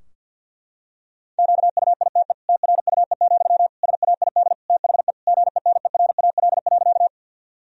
Morse code, heard as a keyed tone, about 48 wpm, 700 Hz. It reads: QUITE TRUE, SAID THE GENERAL.
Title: Sentences from Top 1000 Words - Review Part 3 - 50wpm